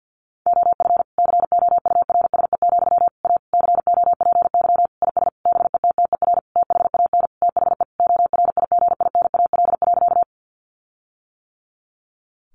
Morse code, heard as a keyed tone, about 50 wpm, 700 Hz. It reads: OF COURSE, A COPY IS BETTER THAN THE ORIGINAL.
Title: Sentences from Top 1400 Words 50wpm